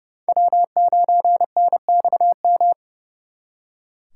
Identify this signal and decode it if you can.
Morse code, heard as a keyed tone, about 38 wpm, 700 Hz. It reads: W9NXM